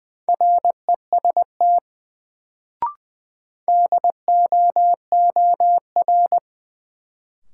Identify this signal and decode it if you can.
Morse code, heard as a keyed tone, about 20 wpm, 700 Hz. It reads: REST DOOR
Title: Sets of 2 Words - New Word Drill - 20wpm